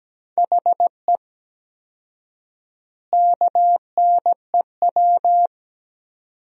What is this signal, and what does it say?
Morse code, heard as a keyed tone, about 17 wpm, 700 Hz. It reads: HE KNEW
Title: 4x Word-Spacing - Sentences from Top 500 Words - 17wpm